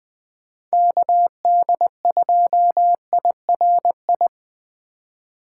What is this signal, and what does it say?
Morse code, heard as a keyed tone, about 20 wpm, 700 Hz. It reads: KD2IRI